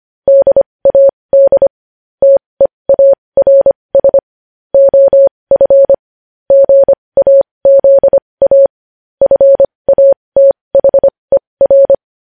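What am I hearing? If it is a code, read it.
Morse code, heard as a keyed tone, about 25 wpm, spaced as for 20 wpm, 550 Hz. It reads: DAD TEARS OF GAZA FATHER
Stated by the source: Morse code at 20 wpm.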